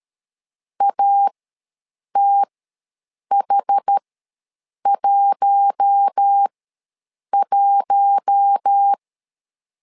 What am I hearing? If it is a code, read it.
Morse code, heard as a keyed tone, about 13 wpm, spaced as for 7 wpm, 800 Hz. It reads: ATH11
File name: LEZIONI / 4 Q1 ws 12wpm 2129